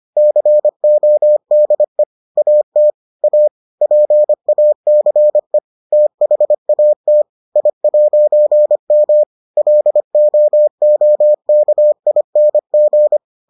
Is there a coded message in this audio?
Yes: Morse code from 0.2 to 13.2 s, heard as a keyed tone, about 25 wpm, 600 Hz. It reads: CODE AT A PACE THAT I'M LOOKING